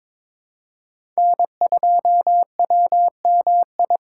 Morse code, heard as a keyed tone, about 22 wpm, 700 Hz. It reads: N2WMI
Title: USA Call Signs - 4x Character Spacing - 22wpm